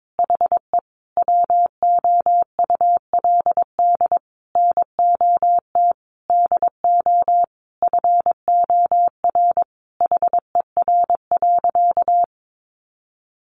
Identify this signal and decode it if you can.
Morse code, heard as a keyed tone, about 22 wpm, 700 Hz. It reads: HE WOULD NOT DO FOR HER.
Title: Sentences from Top 600 Words 22wpm